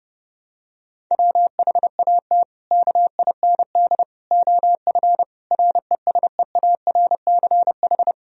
Morse code, heard as a keyed tone, about 28 wpm, 700 Hz. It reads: WHAT KIND OF RESEARCH
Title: Sentences from Top 1000 Words - Review Part 4 - 30wpm